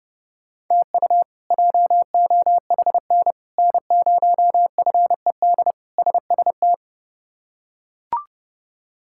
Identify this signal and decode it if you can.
Morse code, heard as a keyed tone, about 30 wpm, 700 Hz. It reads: TU JOHN N0FED SST
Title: K1USN SST QSO Practice - 30wpm